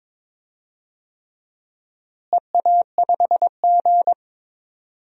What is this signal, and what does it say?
Morse code, heard as a keyed tone, about 22 wpm, 700 Hz. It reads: EA5G
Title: DXCC-100 Call Signs - 4 Characters - Repeated 3x - 22wpm